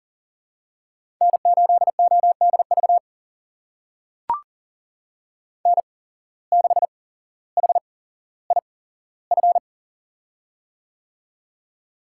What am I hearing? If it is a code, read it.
Morse code, heard as a keyed tone, about 40 wpm, spaced as for 17 wpm, 700 Hz. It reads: N8ODV N6HIF
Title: USA Call Signs - 5 Characters - 8x Character Spacing - 40wpm